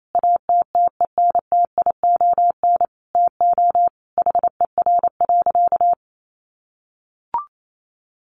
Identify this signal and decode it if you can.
Morse code, heard as a keyed tone, about 28 wpm, 700 Hz. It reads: ATTENTION TO HER.
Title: Sentences from Top 1000 Words 28wpm